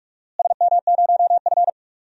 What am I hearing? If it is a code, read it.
Morse code, heard as a keyed tone, about 46 wpm, 700 Hz. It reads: SM0F
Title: DXCC-100 Call Signs - 4 Characters - Repeated 2x - 45wpm